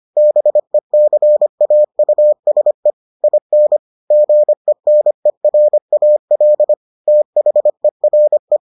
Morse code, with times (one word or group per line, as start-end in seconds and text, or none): 0.2-2.9 BECAUSE
3.2-3.8 IN
4.1-6.7 GENERAL
7.1-8.6 THERE